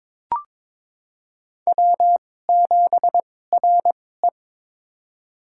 Morse code, heard as a keyed tone, about 22 wpm, spaced as for 16 wpm, 700 Hz. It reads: W7RE
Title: USA Call Signs - 2x Character Spacing - 22wpm